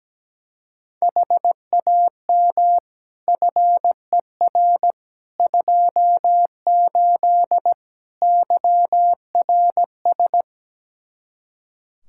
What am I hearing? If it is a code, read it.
Morse code, heard as a keyed tone, about 17 wpm, 700 Hz. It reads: HAM FER 28 YRS
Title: Practice QSO - Exchange 3 - More Info - 17wpm